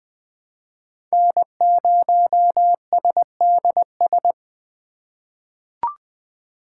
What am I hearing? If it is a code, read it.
Morse code, heard as a keyed tone, about 20 wpm, 700 Hz. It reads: N0SDS
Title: USA Call Signs - 5 Characters - 4x Character Spacing - 20wpm